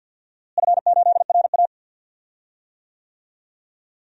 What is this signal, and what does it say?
Morse code, heard as a keyed tone, about 47 wpm, 700 Hz. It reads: F8RA
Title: DXCC-100 Call Signs - 4 Characters - Repeated 3x - 50wpm